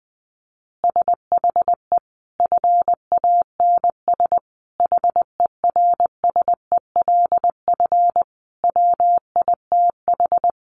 Morse code, heard as a keyed tone, about 20 wpm, 700 Hz. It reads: SHE FANS HERSELF WITH
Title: Additional Easy Phrases 20wpm